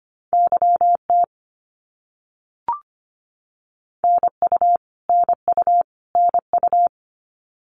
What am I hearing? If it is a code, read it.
Morse code, heard as a keyed tone, about 24 wpm, 700 Hz. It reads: YT NU NU NU
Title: Canadian Provinces and Territories - Repeated 3x - 25wpm